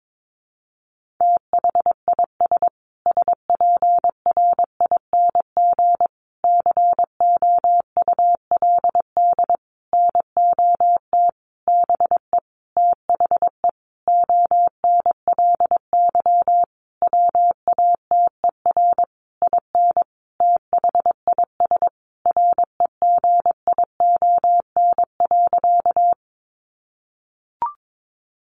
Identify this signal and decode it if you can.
Morse code, heard as a keyed tone, about 22 wpm, 700 Hz. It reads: THIS SPRING COULD NOT BE THE ONLY WATER IN THIS REGION.
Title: Sentences from Top 1500 Words 22wpm